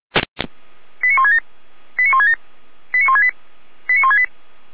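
Five-tone selective call (ZVEI), heard as quick runs of five tones, tones of about 70 ms. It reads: 89167 89167 89168 89168
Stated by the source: ZVEI five-tone selective call 89166 received via SDR.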